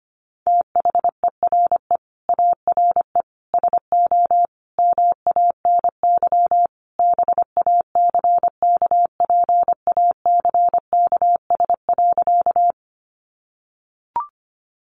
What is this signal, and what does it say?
Morse code, heard as a keyed tone, about 25 wpm, 700 Hz. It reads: THERE ARE SO MANY BACKPACKS.
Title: Additional Easy Phrases 25wpm